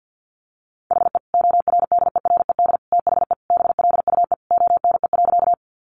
Morse code, heard as a keyed tone, about 50 wpm, 700 Hz. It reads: HE ORDERED THE BLUE ONE.
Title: Additional Easy Phrases 50wpm